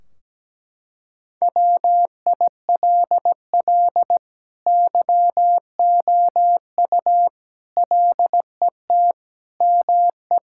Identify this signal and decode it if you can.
Morse code, heard as a keyed tone, about 17 wpm, 700 Hz. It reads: WILL YOU LET ME
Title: Sentences from Top 1600 Words 17wpm